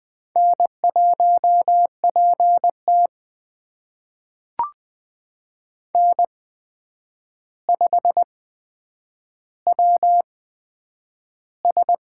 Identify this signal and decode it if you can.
Morse code, heard as a keyed tone, about 20 wpm, spaced as for 9 wpm, 700 Hz. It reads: N1PT N5WS
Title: USA Call Signs - 4 Characters - 8x Character Spacing - 20wpm